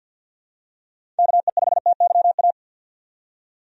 Morse code, heard as a keyed tone, about 52 wpm, 700 Hz. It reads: KE5TYA
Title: USA Call Signs - 6 Characters - 50wpm